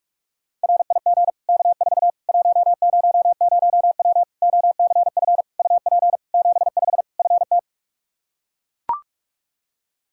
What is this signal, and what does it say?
Morse code, heard as a keyed tone, about 42 wpm, 700 Hz. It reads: RIG K4 100W OCF UP 75 FT
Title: Practice QSO - Exchange 2 - Info - 45wpm